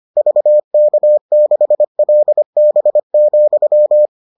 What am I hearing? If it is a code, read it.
Morse code, heard as a keyed tone, about 25 wpm, 600 Hz. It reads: VK6LB,